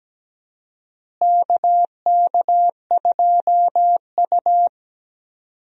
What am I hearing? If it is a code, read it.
Morse code, heard as a keyed tone, about 17 wpm, 700 Hz. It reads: KK2U
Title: USA Call Signs - 4 Characters - Repeated 2x - 17wpm